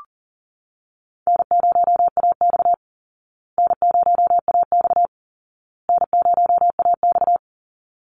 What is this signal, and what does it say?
Morse code, heard as a keyed tone, about 40 wpm, 700 Hz. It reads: N0AX N0AX N0AX
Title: Easy Call Signs Repeated 3x - 40wpm